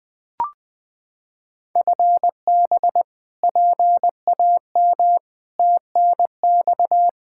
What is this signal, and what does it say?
Morse code, heard as a keyed tone, about 21 wpm, 700 Hz. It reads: FB PAM TNX
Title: Practice QSO - Exchange 3 - More Info - 20wpm